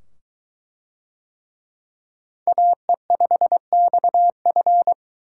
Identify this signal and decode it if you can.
Morse code, heard as a keyed tone, about 20 wpm, 700 Hz. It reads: AE5XF